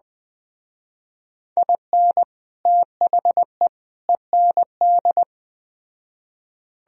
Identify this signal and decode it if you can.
Morse code, heard as a keyed tone, about 20 wpm, 700 Hz. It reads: IN THE END
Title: Speed-Racing - Sets of 3 Words 17wpm